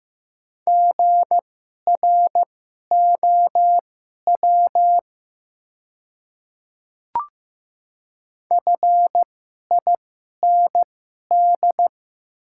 Morse code, heard as a keyed tone, about 15 wpm, spaced as for 11 wpm, 700 Hz. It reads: GROW FIND
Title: 2x Character Spacing - Top 500 Words - 15wpm